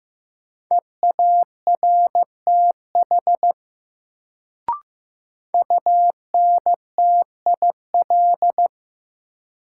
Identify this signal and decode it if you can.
Morse code, heard as a keyed tone, about 15 wpm, 700 Hz. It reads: EARTH UNTIL
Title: Sets of 5 Words Encore - New Word Drill - 15wpm